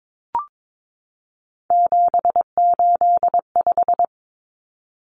Morse code, heard as a keyed tone, about 22 wpm, 700 Hz. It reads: 785
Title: Numbers - 3-Digit 22wpm